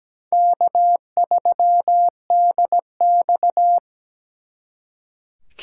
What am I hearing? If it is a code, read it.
Morse code, heard as a keyed tone, about 20 wpm, 700 Hz. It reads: K3DX
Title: Speed Racing - USA 1x2 Call Signs 17wpm